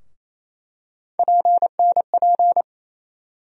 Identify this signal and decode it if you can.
Morse code, heard as a keyed tone, about 28 wpm, 700 Hz. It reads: PNP
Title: USA Call Sign Postfixes - 3 Letters 28wpm